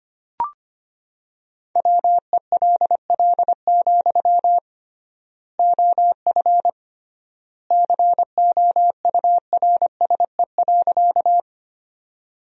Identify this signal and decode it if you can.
Morse code, heard as a keyed tone, about 25 wpm, 700 Hz. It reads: WELL, OF COURSE.
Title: 3x Word-Spacing - Sentences from Top 500 Words - 25wpm